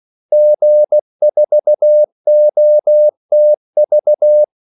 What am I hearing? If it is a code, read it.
Morse code, heard as a keyed tone, about 16 wpm, 600 Hz. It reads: G4OTV
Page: Kaldesignaler 16 wpm